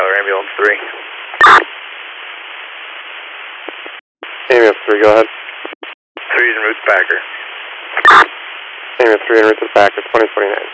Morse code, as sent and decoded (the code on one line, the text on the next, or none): none